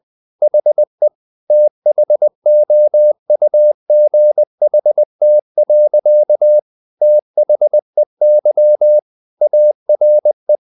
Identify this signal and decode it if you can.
Morse code, heard as a keyed tone, about 20 wpm, 600 Hz. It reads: HE THOUGHT. THEY ARE
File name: the old man and the sea / the old man and the sea part 1 20wpm